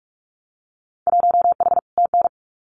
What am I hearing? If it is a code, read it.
Morse code, heard as a keyed tone, about 46 wpm, 700 Hz. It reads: 1H TN
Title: Winter Field Day Exchanges 45wpm